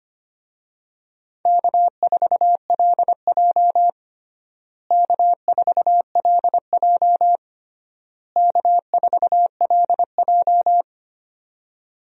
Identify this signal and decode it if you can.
Morse code, heard as a keyed tone, about 25 wpm, 700 Hz. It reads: K4LJ K4LJ K4LJ
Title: USA Call Signs - 4 Characters - Repeated 3x - 25wpm